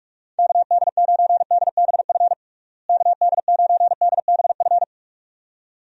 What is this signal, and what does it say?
Morse code, heard as a keyed tone, about 46 wpm, 700 Hz. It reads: KD9DBF KD9DBF
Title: USA Call Signs - 6 Characters - Repeated 2x - 45wpm